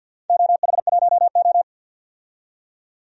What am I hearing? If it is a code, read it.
Morse code, heard as a keyed tone, about 51 wpm, 700 Hz. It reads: OH1O
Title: DXCC-100 Call Signs - 4 Characters - 50wpm